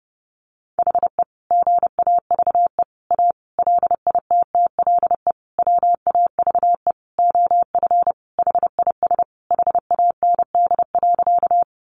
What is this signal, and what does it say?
Morse code, heard as a keyed tone, about 30 wpm, 700 Hz. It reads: HE GAVE A LITTLE WAVE OF HIS HAND.
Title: Sentences from Top 1400 Words 30wpm